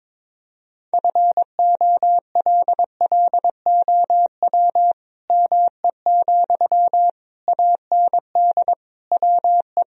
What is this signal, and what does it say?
Morse code, heard as a keyed tone, about 22 wpm, 700 Hz. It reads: FOLLOW ME, AND WE
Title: Sentences from Top 1000 Words - Review Part 3 - 22wpm